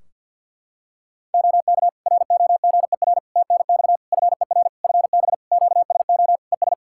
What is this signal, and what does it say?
Morse code, heard as a keyed tone, about 50 wpm, 700 Hz. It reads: OK ROGER TNX FER FB QSO ES
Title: Practice QSO - Exchange 4 - Ending - 50wpm